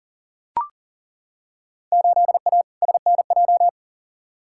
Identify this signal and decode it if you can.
Morse code, heard as a keyed tone, about 40 wpm, 700 Hz. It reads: 8A SNJ
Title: Field Day Exchanges Encore - 40wpm